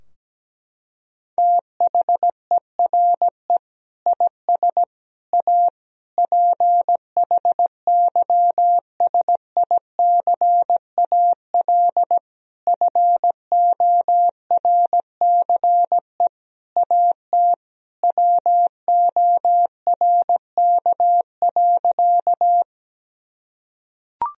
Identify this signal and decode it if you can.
Morse code, heard as a keyed tone, about 17 wpm, 700 Hz. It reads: THERE IS A PHYSICAL FORCE AT WORK.
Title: Sentences from Top 1000 Words - Review Part 4 - 17wpm